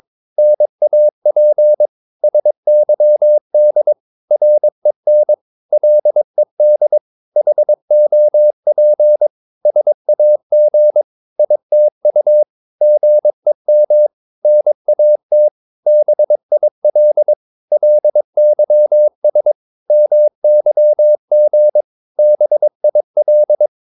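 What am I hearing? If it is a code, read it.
Morse code, heard as a keyed tone, about 22 wpm, 600 Hz. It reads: NAP SYD REN LED HOP SAG ITU GEM NAT BIL LYS MYG BIL
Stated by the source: Morse practice of short Danish words at 22 wpm, from CW med Gnister.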